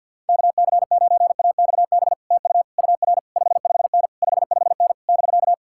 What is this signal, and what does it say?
Morse code, heard as a keyed tone, about 66 wpm, spaced as for 58 wpm, 700 Hz. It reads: KC9AXB TU UR 55N 55N <BK>
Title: Speed-Racing Common QSO Phrases Encore - 50wpm